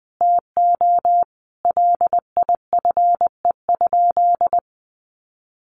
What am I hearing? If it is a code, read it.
Morse code, heard as a keyed tone, about 20 wpm, 700 Hz. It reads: TO LIFE?
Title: Sentences from Top 1500 Words 20wpm